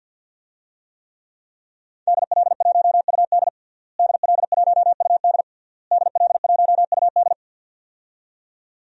Morse code, heard as a keyed tone, about 48 wpm, 700 Hz. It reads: DL1UD DL1UD DL1UD
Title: DXCC-100 Call Signs - 5 Characters - Repeated 3x - 50wpm